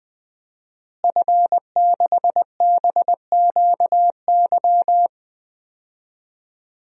Morse code, heard as a keyed tone, about 21 wpm, 700 Hz. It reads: F6BQY
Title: DXCC-100 Call Signs - 5 Characters - Repeated 3x - 20wpm